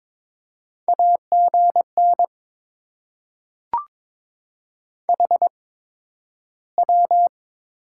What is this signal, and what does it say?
Morse code, heard as a keyed tone, about 22 wpm, spaced as for 12 wpm, 700 Hz. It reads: AGN HW
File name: Top 20 QSO Elements - 8x Character Spacing - 22wpm